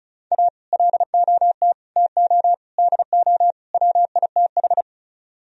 Morse code, heard as a keyed tone, about 45 wpm, 700 Hz. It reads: A LOT TO DO WITH